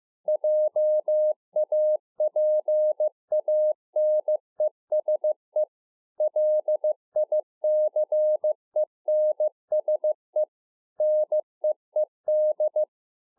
Morse code, heard as a keyed tone, about 15 wpm, 610 Hz. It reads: JAPANESE LICENSE NEED